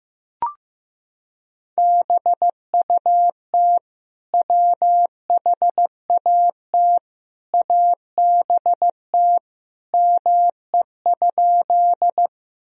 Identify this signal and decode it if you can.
Morse code, heard as a keyed tone, about 15 wpm, 700 Hz. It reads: BUT WHAT ABT ME?
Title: Sentences from Top 200 Words and Abbreviations 15wpm